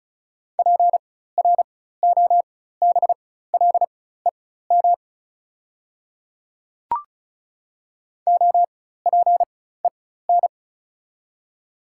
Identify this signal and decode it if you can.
Morse code, heard as a keyed tone, about 36 wpm, spaced as for 16 wpm, 700 Hz. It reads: PROBLEM OPEN
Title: 4x Character Spacing - Top 500 Words - 35wpm